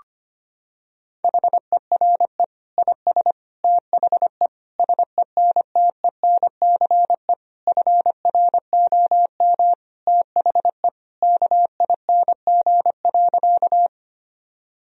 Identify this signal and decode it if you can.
Morse code, heard as a keyed tone, about 25 wpm, 700 Hz. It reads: HERE IS THE SENTENCE FROM THE KING.
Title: Sentences from Top 1000 Words - Review Part 1 - 25wpm